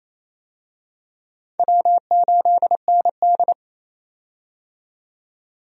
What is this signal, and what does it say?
Morse code, heard as a keyed tone, about 27 wpm, 700 Hz. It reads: W8ND
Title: USA Call Signs - 4 Characters - 28wpm